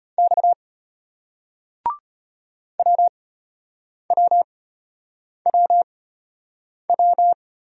Morse code, heard as a keyed tone, about 34 wpm, spaced as for 8 wpm, 700 Hz. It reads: X WWWW